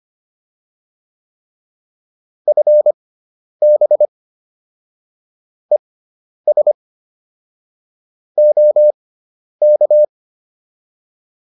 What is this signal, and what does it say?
Morse code, heard as a keyed tone, about 25 wpm, spaced as for 10 wpm, 600 Hz. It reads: FB ES OK